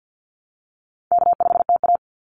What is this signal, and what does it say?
Morse code, heard as a keyed tone, about 50 wpm, 700 Hz. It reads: K5TA
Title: Call Signs Repeated 2x - 50wpm